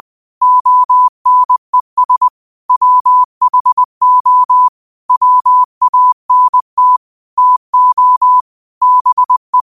Morse code, heard as a keyed tone, about 20 wpm, 1000 Hz. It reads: ONES WHO WANT TO BE